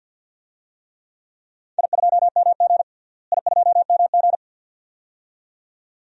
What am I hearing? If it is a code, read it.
Morse code, heard as a keyed tone, about 52 wpm, 700 Hz. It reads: I2MG I2MG
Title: DXCC-100 Call Signs - 4 Characters - Repeated 2x - 50wpm